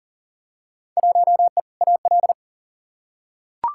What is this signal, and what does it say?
Morse code, heard as a keyed tone, about 40 wpm, 700 Hz. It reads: 1E AL
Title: Field Day Exchanges - 40wpm